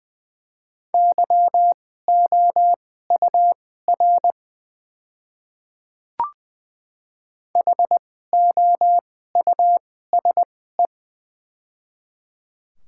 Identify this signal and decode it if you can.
Morse code, heard as a keyed tone, about 20 wpm, spaced as for 15 wpm, 700 Hz. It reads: YOUR HOUSE
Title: 2x Character Spacing - Top 200 Words - 20wpm